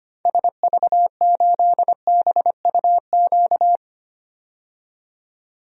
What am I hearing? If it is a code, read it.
Morse code, heard as a keyed tone, about 25 wpm, 700 Hz. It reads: SV8BUQ
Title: DXCC-100 Call Signs - 6 Characters - 25wpm